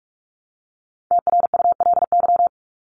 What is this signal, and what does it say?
Morse code, heard as a keyed tone, about 44 wpm, 700 Hz. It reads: TRULY
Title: Top 1501-1750 Words 45wpm